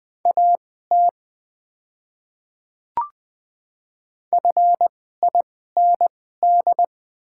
Morse code, heard as a keyed tone, about 20 wpm, spaced as for 15 wpm, 700 Hz. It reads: AT FIND
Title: 2x Character Spacing - Top 100 Words - 20wpm